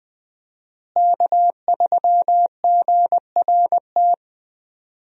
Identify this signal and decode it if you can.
Morse code, heard as a keyed tone, about 20 wpm, 700 Hz. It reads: K3GRT
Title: USA Call Signs - 5 Characters - Repeated 3x - 20wpm